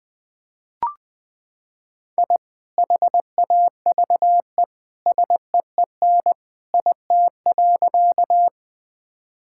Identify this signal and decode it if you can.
Morse code, heard as a keyed tone, about 21 wpm, 700 Hz. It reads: I HAVE SEEN IT.
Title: Sentences with 2-Word Phrases - 20wpm